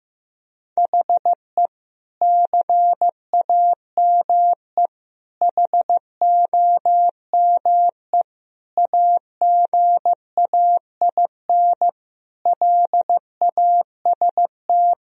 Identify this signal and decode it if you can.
Morse code, heard as a keyed tone, about 15 wpm, 700 Hz. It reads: HE CAME HOME AGAIN LAST
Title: Sentences from Top 1000 Words - Review Part 2 - 15wpm